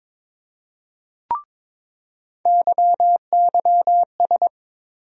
Morse code, heard as a keyed tone, about 22 wpm, 700 Hz. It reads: YYS
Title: USA Call Sign Postfixes - 3 Letters 22wpm